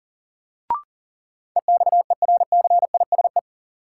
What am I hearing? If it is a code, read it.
Morse code, heard as a keyed tone, about 40 wpm, 700 Hz. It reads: EXERCISE
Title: Sentences from Top 1200 Words - New Word Drill Alt - 40wpm